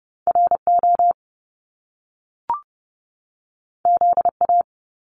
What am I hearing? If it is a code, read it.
Morse code, heard as a keyed tone, about 30 wpm, 700 Hz. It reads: RO ZA
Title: Top Two Letter Combinations in Words 30wpm